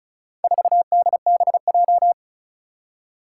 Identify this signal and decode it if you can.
Morse code, heard as a keyed tone, about 36 wpm, 700 Hz. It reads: 4DBJ
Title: Four Challenging Letter-Number Combinations 35wpm